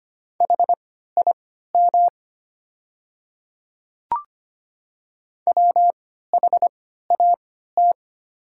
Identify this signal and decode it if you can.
Morse code, heard as a keyed tone, about 25 wpm, spaced as for 14 wpm, 700 Hz. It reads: HIM WHAT
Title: 3x Character Spacing - Top 100 Words - 25wpm